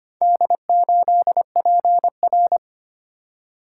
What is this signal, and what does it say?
Morse code, heard as a keyed tone, about 25 wpm, 700 Hz. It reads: D8PR
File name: practice / 25wpm / Four Letter-Number Combinations - 25wpm